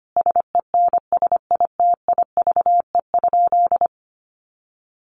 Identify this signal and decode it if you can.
Morse code, heard as a keyed tone, about 25 wpm, 700 Hz. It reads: SENSITIVE?
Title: Sentences from Top 1400 Words 25wpm